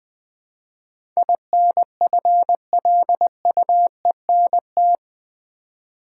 Morse code, heard as a keyed tone, about 20 wpm, 700 Hz. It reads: INFLUENT
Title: Build A Word - 4 Letters At A Time - 11-Letter Words - 20wpm